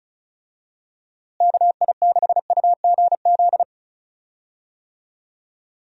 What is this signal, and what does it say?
Morse code, heard as a keyed tone, about 36 wpm, 700 Hz. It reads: KI6UGZ
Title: USA Call Signs - 6 Characters - Repeated 2x - 35wpm